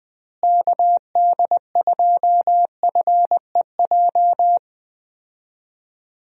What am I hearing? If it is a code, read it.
Morse code, heard as a keyed tone, about 20 wpm, 700 Hz. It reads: KD2FEJ
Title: USA Call Signs - 6 Characters - Repeated 2x - 20wpm